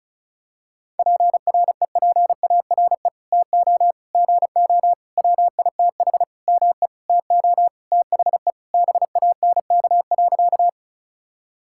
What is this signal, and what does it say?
Morse code, heard as a keyed tone, about 36 wpm, 700 Hz. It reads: PREPARE TO GO WITH ME TO THE BANK.
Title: Sentences from Top 1000 Words - Review Part 6 - 35wpm